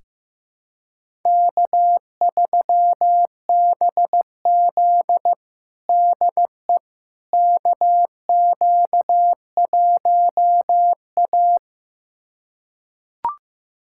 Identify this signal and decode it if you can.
Morse code, heard as a keyed tone, about 15 wpm, 700 Hz. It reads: K3BZ DE KQ1A
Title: USA Callsign DE Callsign - 4 Characters - 15wpm